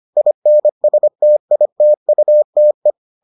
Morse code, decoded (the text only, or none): INSTITUTE